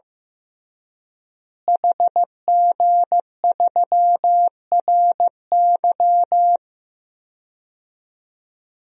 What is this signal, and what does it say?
Morse code, heard as a keyed tone, about 15 wpm, 700 Hz. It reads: HG3RY
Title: DXCC-100 Call Signs - 5 Characters - Repeated 3x - 15wpm